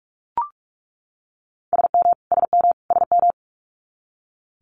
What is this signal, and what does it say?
Morse code, heard as a keyed tone, about 46 wpm, 690 Hz. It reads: SM SM SM
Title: Multi-letter Phonemes Repeated 3x 45wpm